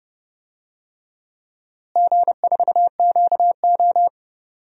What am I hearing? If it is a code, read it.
Morse code, heard as a keyed tone, about 32 wpm, 700 Hz. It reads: G4QO
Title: DXCC-100 Call Signs - 4 Characters - Repeated 3x - 30wpm